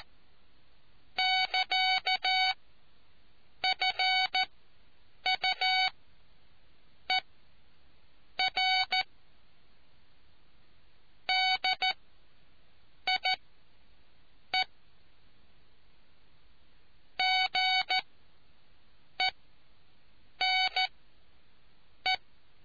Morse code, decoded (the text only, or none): <KA> FUER DIE GENE